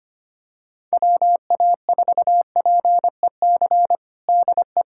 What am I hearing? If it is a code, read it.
Morse code, heard as a keyed tone, about 25 wpm, 700 Hz. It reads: WA4PEC DE
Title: USA Callsign DE Callsign - 6 Characters - 25wpm